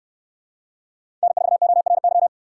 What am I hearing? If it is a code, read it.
Morse code, heard as a keyed tone, about 68 wpm, 700 Hz. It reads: N4ORY